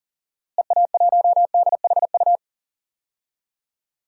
Morse code, heard as a keyed tone, about 40 wpm, 700 Hz. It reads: EA1DHU
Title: DXCC-100 Call Signs - 6 Characters - Repeated 3x - 40wpm